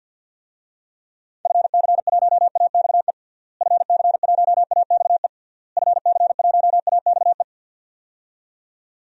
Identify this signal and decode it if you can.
Morse code, heard as a keyed tone, about 51 wpm, 700 Hz. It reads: FC1AXE FC1AXE FC1AXE
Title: DXCC-100 Call Signs - 6 Characters - Repeated 3x - 50wpm